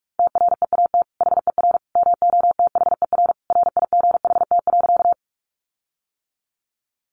Morse code, heard as a keyed tone, about 44 wpm, 700 Hz. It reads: TREAT HER MOTHER RIGHT.
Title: Sentences from Top 1300 Words 45wpm